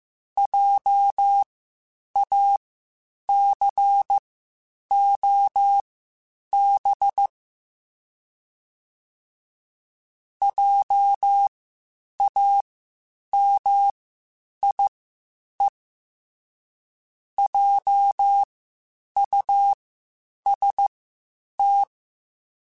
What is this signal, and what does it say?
Morse code, heard as a keyed tone, about 15 wpm, spaced as for 8 wpm, 800 Hz. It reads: JACOB JAMIE JUST
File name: Spaced Alphas / Spaced Alphas mp3s / Js sp @13